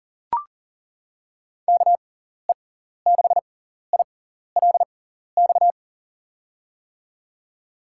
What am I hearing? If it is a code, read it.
Morse code, heard as a keyed tone, about 40 wpm, spaced as for 14 wpm, 700 Hz. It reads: KE6ILX
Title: USA Call Signs - 6 Characters - 6x Character Spacing - 40wpm